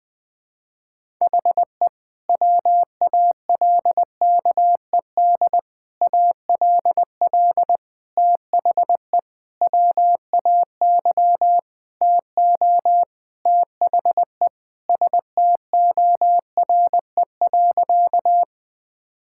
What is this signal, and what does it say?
Morse code, heard as a keyed tone, about 20 wpm, 700 Hz. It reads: HE WALKED ALL THE WAY TO THE STORE.
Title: Sentences with 5-Word Phrases - 20wpm